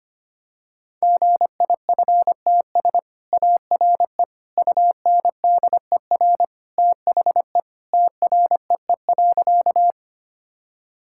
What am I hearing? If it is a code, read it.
Morse code, heard as a keyed tone, about 25 wpm, 700 Hz. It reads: GIFTS ARE UNDER THE TREE.
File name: Easy Sentences with Hints - 25wpm